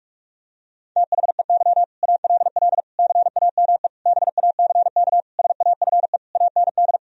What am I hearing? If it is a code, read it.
Morse code, heard as a keyed tone, about 41 wpm, 700 Hz. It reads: THEY ALL CAME BACK SAFE AND
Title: Sentences from Top 1000 Words - Review Part 1 - 45wpm